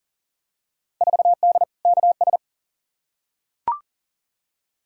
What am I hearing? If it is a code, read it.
Morse code, heard as a keyed tone, about 40 wpm, 700 Hz. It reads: 4D KS
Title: Field Day Exchanges Encore - 40wpm